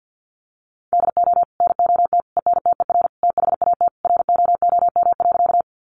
Morse code, heard as a keyed tone, about 50 wpm, 700 Hz. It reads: DO NOT ENTER THAT ROOM.
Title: Sentences from Top 1300 Words 50wpm